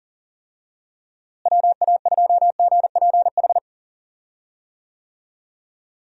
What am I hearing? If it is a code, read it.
Morse code, heard as a keyed tone, about 40 wpm, 700 Hz. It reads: WA2GPH
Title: USA Call Signs - 6 Characters - Repeated 3x - 40wpm